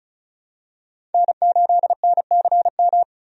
Morse code, heard as a keyed tone, about 34 wpm, 700 Hz. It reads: N8NCM
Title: USA Call Signs - 5 Characters - 35wpm